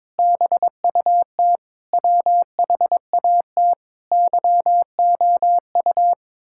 Morse code, heard as a keyed tone, about 22 wpm, 700 Hz. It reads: BUT WHAT YOU
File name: Sentences with 9-Word Phrases - 22wpm